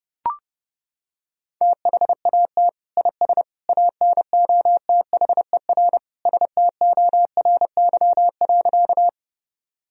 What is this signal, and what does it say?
Morse code, heard as a keyed tone, about 30 wpm, 700 Hz. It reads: THAT IS ANOTHER STORY.
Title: Sentences from Top 1000 Words - Review Part 1 - 30wpm